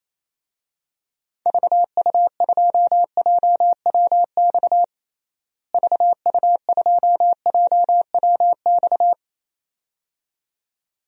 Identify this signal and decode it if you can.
Morse code, heard as a keyed tone, about 28 wpm, 700 Hz. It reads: VU2JWX VU2JWX
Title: DXCC-100 Call Signs - 6 Characters - Repeated 2x - 28wpm